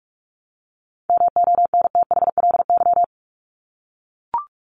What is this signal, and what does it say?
Morse code, heard as a keyed tone, about 44 wpm, 700 Hz. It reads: MONTHLY
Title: LY Suffix - Top 100 Words with Base Words 45wpm